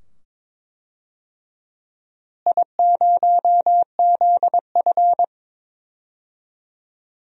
Morse code, heard as a keyed tone, about 21 wpm, 700 Hz. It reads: I0ZF
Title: DXCC-100 Call Signs - 4 Characters - Repeated 2x - 22wpm